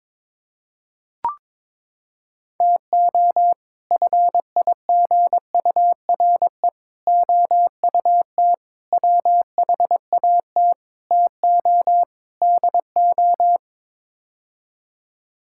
Morse code, heard as a keyed tone, about 22 wpm, 700 Hz. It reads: TO FIGURE OUT WHAT TO DO
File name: practice / 22wpm / Sets of 6 Words - 22wpm